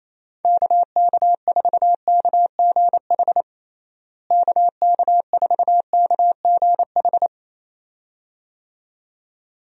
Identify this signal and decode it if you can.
Morse code, heard as a keyed tone, about 27 wpm, 700 Hz. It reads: KK4KGH KK4KGH
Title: USA Call Signs - 6 Characters - Repeated 2x - 28wpm